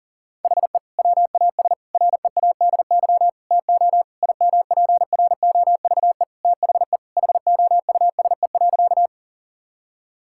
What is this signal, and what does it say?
Morse code, heard as a keyed tone, about 40 wpm, 700 Hz. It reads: HE WAS READY TO IMPROVE THE HOUSE.
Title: Sentences from Top 1000 Words - Review Part 5 - 40wpm